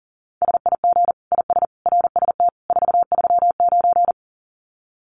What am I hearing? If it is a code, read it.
Morse code, heard as a keyed tone, about 40 wpm, 700 Hz. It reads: SIG IS RST 439
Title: Common QSO Phrases 40wpm